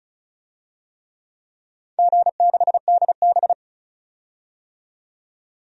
Morse code, heard as a keyed tone, about 36 wpm, 700 Hz. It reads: G6DB
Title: DXCC-100 Call Signs - 4 Characters - 35wpm